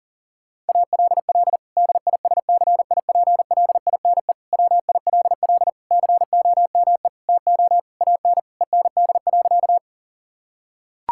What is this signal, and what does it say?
Morse code, heard as a keyed tone, about 40 wpm, 700 Hz. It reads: ALL DISCIPLINE WILL COME TO AN END.
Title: Sentences from Top 1000 Words - Review Part 6 - 40wpm